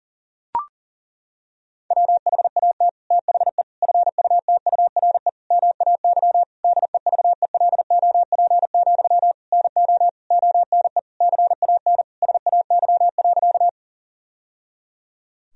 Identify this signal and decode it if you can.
Morse code, heard as a keyed tone, about 40 wpm, 700 Hz. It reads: WHAT THE FUTURE MAY DEVELOP, NO ONE CAN SAY.
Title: Sentences from Top 1000 Words - Review Part 2 - 40wpm